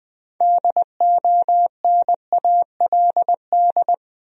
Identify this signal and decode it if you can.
Morse code, heard as a keyed tone, about 20 wpm, 700 Hz. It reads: DONALD
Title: USA Common Names - 20wpm